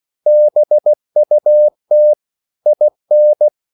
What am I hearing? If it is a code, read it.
Morse code, heard as a keyed tone, about 16 wpm, 600 Hz. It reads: BUT IN
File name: CW / Practice / EnglishQuotations0116